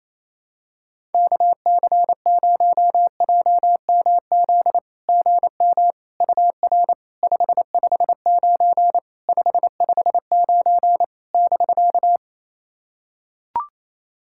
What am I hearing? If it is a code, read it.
Morse code, heard as a keyed tone, about 28 wpm, 700 Hz. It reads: KC0JMZ GM UR 559 559 <BK>
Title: SOTA QSO Practice 28wpm